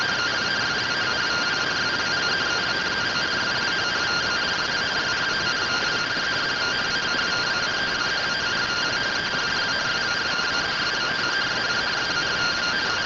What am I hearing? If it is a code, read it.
Two-tone teleprinter signal, about 170 Hz shift.
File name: findings / MR CORONAL fr1 -131.gen-33 rtty rx